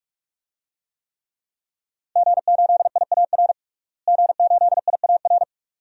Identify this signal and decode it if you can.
Morse code, heard as a keyed tone, about 41 wpm, 700 Hz. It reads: G8IAR G8IAR
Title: DXCC-100 Call Signs - 5 Characters - Repeated 2x - 45wpm